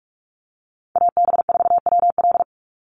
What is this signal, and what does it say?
Morse code, heard as a keyed tone, about 44 wpm, 700 Hz. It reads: AB4WL
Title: NA SOTA Chaser Callsigns 45wpm